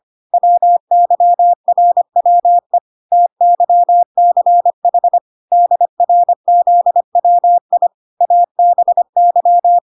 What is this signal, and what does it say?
Morse code, heard as a keyed tone, about 25 wpm, 700 Hz. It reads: WYRWE TYCH DRZWI ABY